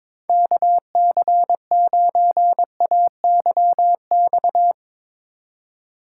Morse code, heard as a keyed tone, about 21 wpm, 700 Hz. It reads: KC9AYX
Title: USA Call Signs - 6 Characters - 2x Character Spacing - 22wpm